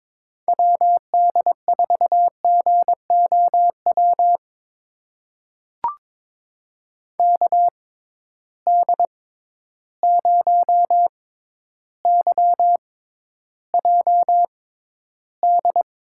Morse code, heard as a keyed tone, about 22 wpm, spaced as for 11 wpm, 700 Hz. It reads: WD4GOW KD0YJD
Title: USA Call Signs - 6x Character Spacing - 22wpm